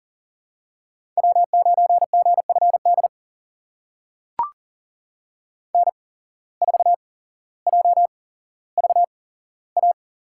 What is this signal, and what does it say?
Morse code, heard as a keyed tone, about 39 wpm, spaced as for 17 wpm, 700 Hz. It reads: W9GFD N4JVA
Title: USA Call Signs - 5 Characters - 8x Character Spacing - 40wpm